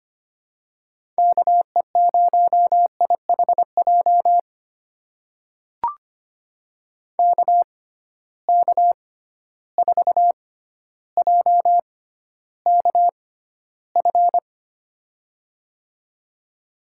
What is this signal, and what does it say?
Morse code, heard as a keyed tone, about 25 wpm, spaced as for 13 wpm, 700 Hz. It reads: KE0IHJ KK4JKF
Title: USA Call Signs - 6 Characters - 6x Character Spacing - 25wpm